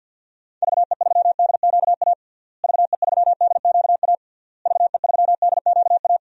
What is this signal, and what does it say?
Morse code, heard as a keyed tone, about 52 wpm, 700 Hz. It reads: VE3DQA VE3DQA VE3DQA
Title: DXCC-100 Call Signs - 6 Characters - Repeated 3x - 50wpm